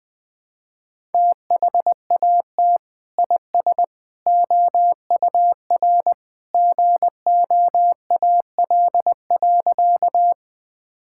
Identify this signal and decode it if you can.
Morse code, heard as a keyed tone, about 20 wpm, 700 Hz. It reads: THAT IS OUR GOAL.
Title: Sentences from Top 1000 Words - Review Part 5 - 20wpm